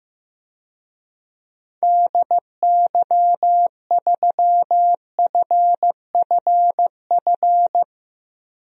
Morse code, heard as a keyed tone, about 15 wpm, 700 Hz. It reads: DY3FFF
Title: DXCC-100 Call Signs - 6 Characters - 15wpm